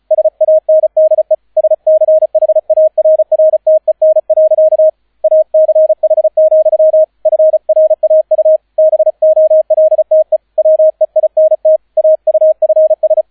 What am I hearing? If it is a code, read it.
Morse code, heard as a keyed tone, about 33 wpm, 630 Hz, with radio noise. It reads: SANDE SCHARRTEN. ACH, FRAU BOLTE WEINT AUFS